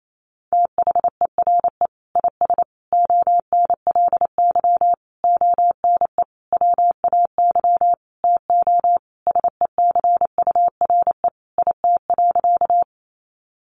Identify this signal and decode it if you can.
Morse code, heard as a keyed tone, about 28 wpm, 700 Hz. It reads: THERE IS ONLY ONE WAY TO SECURE IT.
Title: Sentences from Top 1200 Words 28wpm